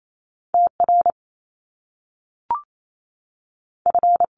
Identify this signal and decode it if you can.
Morse code, heard as a keyed tone, about 28 wpm, 700 Hz. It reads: T R F
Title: Top Two Letter Combinations in Words 28wpm